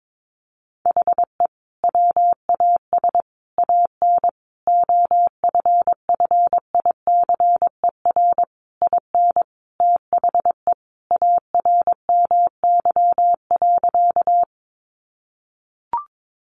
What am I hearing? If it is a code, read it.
Morse code, heard as a keyed tone, about 22 wpm, 700 Hz. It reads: HE WAS AN OFFICER IN THE ARMY.
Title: Sentences from Top 1200 Words 22wpm